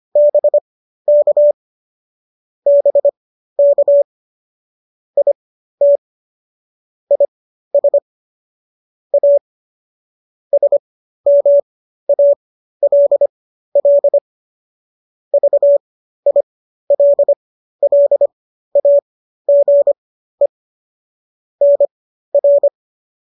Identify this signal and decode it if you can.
Morse code, heard as a keyed tone, about 25 wpm, spaced as for 13 wpm, 570 Hz. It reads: BK BK IT IS A SMALL VILLAGE NR